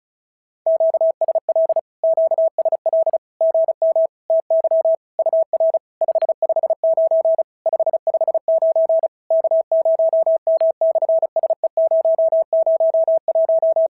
Morse code, heard as a keyed tone, about 36 wpm, 650 Hz. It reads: QSL QSL GM TY UR 559 559 K0M/SE001
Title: SOTA QSO Practice 35wpm